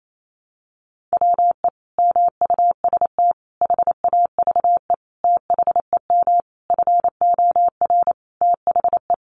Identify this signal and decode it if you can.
Morse code, heard as a keyed tone, about 28 wpm, 700 Hz. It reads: WE MUST HAVE THEM FOR THE
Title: Sentences from Top 1200 Words 28wpm